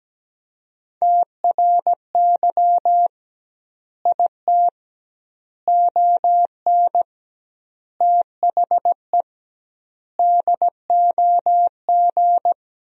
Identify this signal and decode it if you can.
Morse code, heard as a keyed tone, about 17 wpm, 700 Hz. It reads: TRY IT ON THE DOG
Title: 2x Word-Spacing - Sentences from Top 400 Words - 17wpm